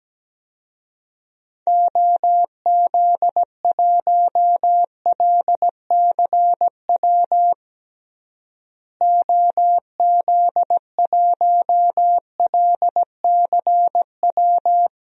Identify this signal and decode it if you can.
Morse code, heard as a keyed tone, about 17 wpm, 700 Hz. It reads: OZ1LCW OZ1LCW
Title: DXCC-100 Call Signs - 6 Characters - Repeated 2x - 17wpm